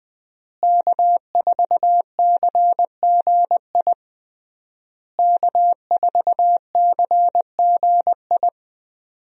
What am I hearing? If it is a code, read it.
Morse code, heard as a keyed tone, about 20 wpm, 700 Hz. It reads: K4CGI K4CGI
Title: USA Call Signs - 5 Characters - Repeated 3x - 20wpm